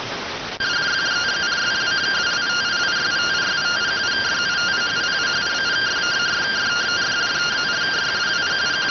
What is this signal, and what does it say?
Two-tone teleprinter signal, about 170 Hz shift.